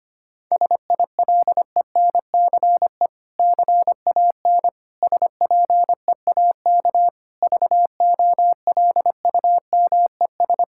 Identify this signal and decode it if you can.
Morse code, heard as a keyed tone, about 25 wpm, 700 Hz. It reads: SILENCE CAN SPEAK VOLUMES
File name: practice / 25wpm / Difficult Sentences with Hints - 25wpm